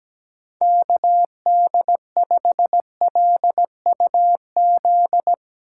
Morse code, heard as a keyed tone, about 18 wpm, 700 Hz. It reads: KD5LUZ
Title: Speed Racing - USA 2x3 Call Signs 17wpm